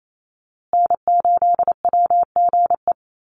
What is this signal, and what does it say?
Morse code, heard as a keyed tone, about 28 wpm, 700 Hz. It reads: N8WGE
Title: USA Call Signs Repeated 3x 28wpm